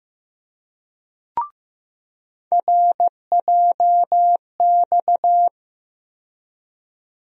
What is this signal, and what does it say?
Morse code, heard as a keyed tone, about 16 wpm, 700 Hz. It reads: RJX
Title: USA Call Sign Postfixes - 3 Letters 15wpm